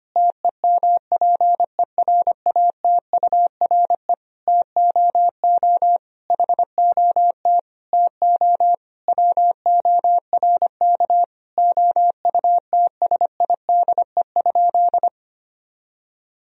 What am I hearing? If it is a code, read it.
Morse code, heard as a keyed tone, about 25 wpm, 700 Hz. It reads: TEMPERATURE TOO HOT TO WORK OUTSIDE?
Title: Sentences from Top 1000 Words - Review Part 4 - 25wpm